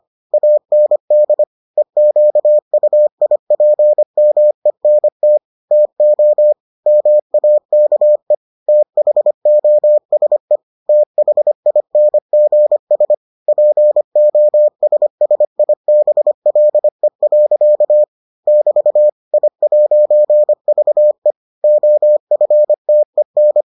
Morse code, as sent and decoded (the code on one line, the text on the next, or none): .- -. -.. / . --.- ..- .. .--. -- . -. - / - --- / -- .- -.- . / - .... --- ... / - .... .. -. --. ... / .--. --- ... ... .. -... .-.. .-.-.- / -...- / .. .----. ...- . / --- ..-. - . -.
AND EQUIPMENT TO MAKE THOSE THINGS POSSIBLE. <BT> I'VE OFTEN